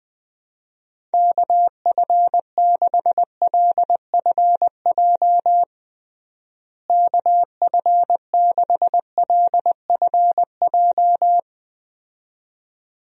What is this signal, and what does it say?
Morse code, heard as a keyed tone, about 20 wpm, 700 Hz. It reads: KF6LFJ KF6LFJ
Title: USA Call Signs - 6 Characters - Repeated 2x - 20wpm